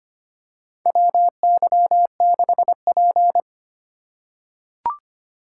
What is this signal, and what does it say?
Morse code, heard as a keyed tone, about 25 wpm, 700 Hz. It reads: WY6P
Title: USA Call Signs - 4 Characters - 3x Character Spacing - 25wpm